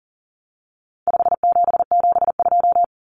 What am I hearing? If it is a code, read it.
Morse code, heard as a keyed tone, about 40 wpm, 700 Hz. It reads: 5772
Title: Common Cut Numbers - 4-Digit 40wpm